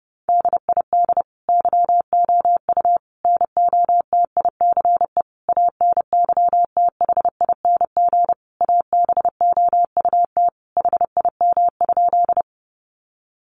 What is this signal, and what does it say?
Morse code, heard as a keyed tone, about 30 wpm, 700 Hz. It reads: DID YOU NOTICE ANYTHING ABOUT HIM?
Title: Sentences from Top 1000 Words 30wpm